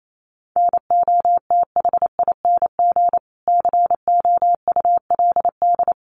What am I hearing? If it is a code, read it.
Morse code, heard as a keyed tone, about 28 wpm, 700 Hz. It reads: NOTHING COULD
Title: Sentences from Top 1400 Words 28wpm